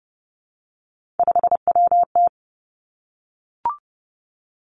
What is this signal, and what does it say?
Morse code, heard as a keyed tone, about 30 wpm, 700 Hz. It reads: 5WT
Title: Three Challenging Letter-Number Combinations 30wpm